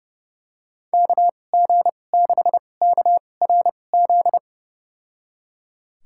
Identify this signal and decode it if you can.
Morse code, heard as a keyed tone, about 30 wpm, spaced as for 22 wpm, 700 Hz. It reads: KG6KRZ
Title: USA Call Signs - 6 Characters - 2x Character Spacing - 30wpm